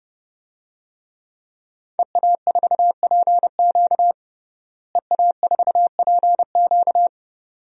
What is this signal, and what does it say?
Morse code, heard as a keyed tone, about 30 wpm, 700 Hz. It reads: EA4PQ EA4PQ
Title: DXCC-100 Call Signs - 5 Characters - Repeated 2x - 30wpm